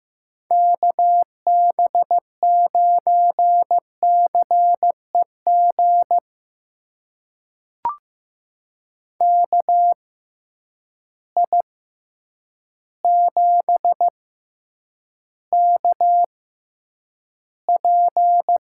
Morse code, heard as a keyed tone, about 15 wpm, spaced as for 8 wpm, 700 Hz. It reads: KB9CEG KI7KP
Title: USA Call Signs - 6 Characters - 6x Character Spacing - 15wpm